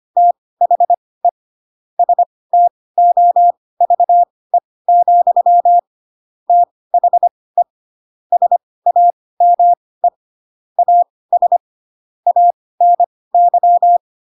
Morse code, heard as a keyed tone, about 25 wpm, spaced as for 18 wpm, 700 Hz. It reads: THE STOVE, THE SAME AS ANY
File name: Pinocchio / Pinocchio ch1 12 25 ik1hge